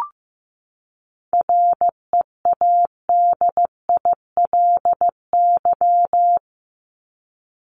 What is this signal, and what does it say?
Morse code, heard as a keyed tone, about 15 wpm, 700 Hz. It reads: READILY
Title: Top 1751-2000 Words 15wpm